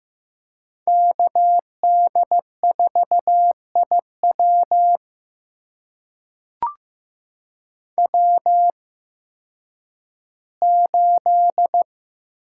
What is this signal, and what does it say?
Morse code, heard as a keyed tone, about 15 wpm, spaced as for 10 wpm, 700 Hz. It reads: KD4IW W8
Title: USA Call Signs - 5 Characters - 8x Character Spacing - 15wpm